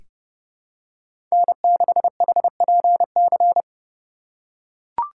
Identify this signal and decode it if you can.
Morse code, heard as a keyed tone, about 32 wpm, 700 Hz. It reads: N6HPC
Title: USA Call Signs - 4x Character Spacing - 30wpm